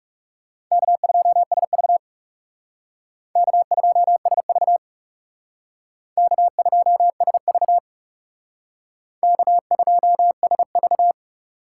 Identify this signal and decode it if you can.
Morse code, heard as a keyed tone, about 38 wpm, 700 Hz. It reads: K2SV K2SV K2SV K2SV